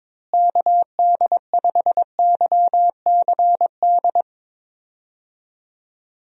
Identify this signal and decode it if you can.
Morse code, heard as a keyed tone, about 22 wpm, 700 Hz. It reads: KD5YCD
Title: USA Call Signs - 6 Characters - Repeated 2x - 22wpm